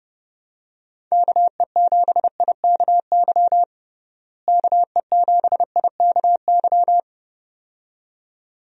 Morse code, heard as a keyed tone, about 29 wpm, 700 Hz. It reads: KE7IKY KE7IKY
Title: USA Call Signs - 6 Characters - Repeated 2x - 30wpm